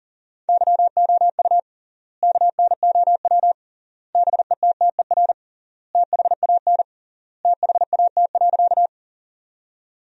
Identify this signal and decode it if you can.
Morse code, heard as a keyed tone, about 40 wpm, 700 Hz. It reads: YOU KNOW BETTER THAN THAT.
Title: 3x Word-Spacing - Sentences from Top 400 Words - 40wpm